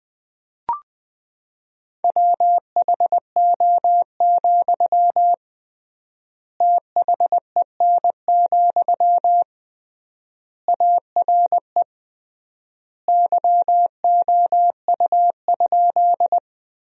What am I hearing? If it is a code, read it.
Morse code, heard as a keyed tone, about 20 wpm, 700 Hz. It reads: WHO, THEN, ARE YOU?
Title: 3x Word-Spacing - Sentences from Top 100 Words - 20wpm